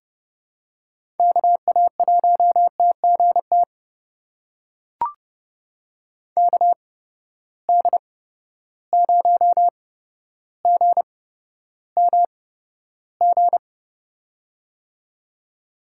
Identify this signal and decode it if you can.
Morse code, heard as a keyed tone, about 30 wpm, spaced as for 13 wpm, 700 Hz. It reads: KA1TGT KD0GMG
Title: USA Call Signs - 6 Characters - 8x Character Spacing - 30wpm